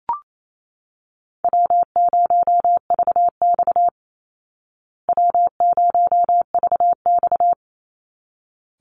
Morse code, heard as a keyed tone, about 28 wpm, 700 Hz. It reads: W0VX W0VX
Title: Easy Call Signs Repeated 2x - 28wpm